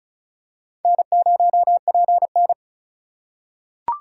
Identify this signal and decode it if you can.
Morse code, heard as a keyed tone, about 34 wpm, 700 Hz. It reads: N0PN
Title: USA Call Signs - 4 Characters - 4x Character Spacing - 35wpm